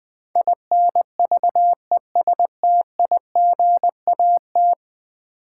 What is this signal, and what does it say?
Morse code, heard as a keyed tone, about 20 wpm, 700 Hz. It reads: INVESTIGAT
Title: Build A Word - 5 Letters At A Time - 12-Letter Words - 20wpm